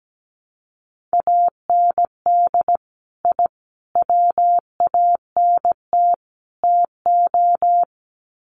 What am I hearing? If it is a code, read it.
Morse code, heard as a keyed tone, about 17 wpm, 700 Hz. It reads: AND I WANT TO
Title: Sets of 4 Words 17wpm